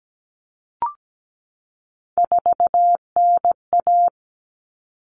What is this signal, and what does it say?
Morse code, heard as a keyed tone, about 17 wpm, 700 Hz. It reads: 4NA
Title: Common Cut Numbers - 3-Digit 17wpm